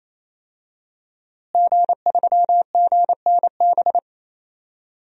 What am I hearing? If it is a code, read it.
Morse code, heard as a keyed tone, about 28 wpm, 700 Hz. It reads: G3GNB
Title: DXCC-100 Call Signs - 5 Characters - Repeated 3x - 28wpm